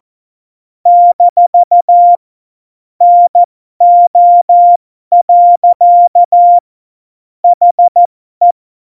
Morse code, heard as a keyed tone, about 14 wpm, spaced as for 12 wpm, 700 Hz. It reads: - NO. HE
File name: Morsekurs / Witze